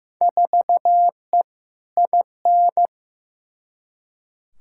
Morse code, heard as a keyed tone, about 15 wpm, 700 Hz. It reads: 4E IN
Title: Field Day Exchanges - 15wpm